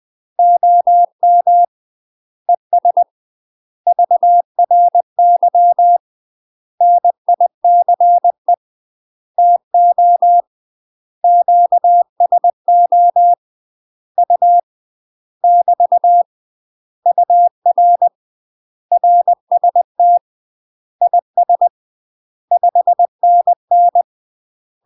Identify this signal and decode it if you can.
Morse code, heard as a keyed tone, about 20 wpm, 700 Hz. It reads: OM ES VRY NICE TO QSO U <BT> UR RST IS 5NN
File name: UK MASTER QSOs TRAINING / 20WPM / UK MASTER QSOs 0059